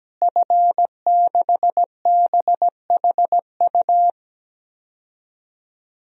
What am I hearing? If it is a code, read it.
Morse code, heard as a keyed tone, about 17 wpm, 700 Hz. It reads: F6BHU
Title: DXCC-100 Call Signs - 5 Characters - Repeated 2x - 17wpm